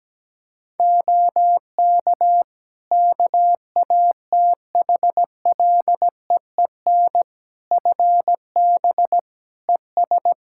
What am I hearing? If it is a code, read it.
Morse code, heard as a keyed tone, about 17 wpm, 700 Hz. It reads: OK KATHLEEN FB ES
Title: Practice QSO - Exchange 2 - Info - 17wpm